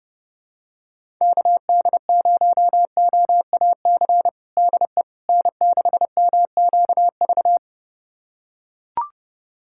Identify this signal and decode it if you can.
Morse code, heard as a keyed tone, about 30 wpm, 700 Hz. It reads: KD0OAC DE N6MQV
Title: USA Callsign DE Callsign 30wpm